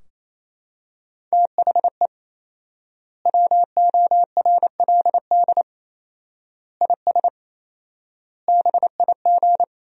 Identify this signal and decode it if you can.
Morse code, heard as a keyed tone, about 27 wpm, 700 Hz. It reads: THE WORLD IS BIG
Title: 4x Word-Spacing - Sentences from Top 400 Words - 28wpm